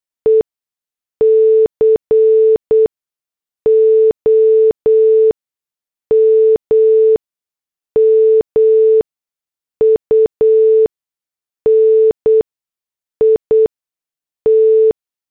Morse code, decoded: E COMMUNIT